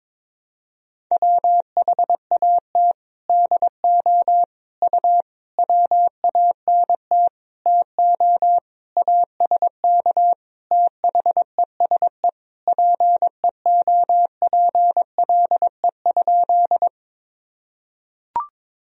Morse code, heard as a keyed tone, about 22 wpm, 700 Hz. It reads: WHAT DO U WANT TO ASK THESE PEOPLE?
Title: Sentences from Top 200 Words and Abbreviations 22wpm